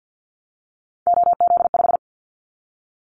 Morse code, heard as a keyed tone, about 50 wpm, 700 Hz. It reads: OZ5
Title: Callsign Prefixes for Top 100 DXCC - 50wpm